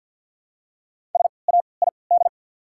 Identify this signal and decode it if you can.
Morse code, heard as a keyed tone, about 48 wpm, spaced as for 28 wpm, 700 Hz. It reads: SAID
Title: 3x Character Spacing - Top 100 Words - 50wpm